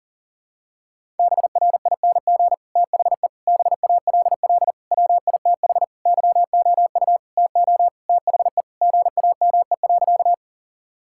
Morse code, heard as a keyed tone, about 37 wpm, 700 Hz. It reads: BRING THE BALL WITH YOU TO THE GAME.
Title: Sentences from Top 1000 Words - Review Part 6 - 40wpm